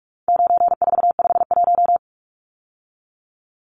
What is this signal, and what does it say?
Morse code, heard as a keyed tone, about 44 wpm, 700 Hz. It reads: 9451
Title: Numbers - 4-Digit 45wpm